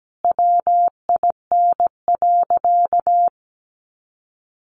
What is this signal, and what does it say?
Morse code, heard as a keyed tone, about 17 wpm, 700 Hz. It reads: WIN.
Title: Sentences from Top 1500 Words 17wpm